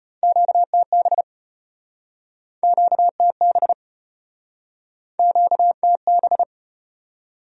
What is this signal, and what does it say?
Morse code, heard as a keyed tone, about 35 wpm, 700 Hz. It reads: QTB QTB QTB